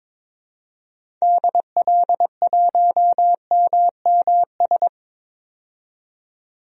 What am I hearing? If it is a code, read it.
Morse code, heard as a keyed tone, about 22 wpm, 700 Hz. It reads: DL1MMS
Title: DXCC-100 Call Signs - 6 Characters - 22wpm